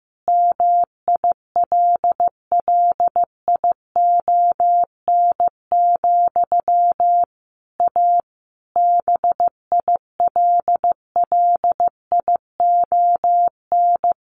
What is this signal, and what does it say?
Morse code, heard as a keyed tone, about 15 wpm, 700 Hz. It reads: MILLION, A BILLION
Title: A Sound Of Thunder 15wpm